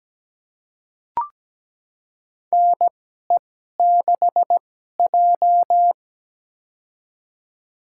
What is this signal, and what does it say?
Morse code, heard as a keyed tone, about 17 wpm, spaced as for 12 wpm, 700 Hz. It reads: NE6J
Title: USA Call Signs - 4 Characters - 2x Character Spacing - 17wpm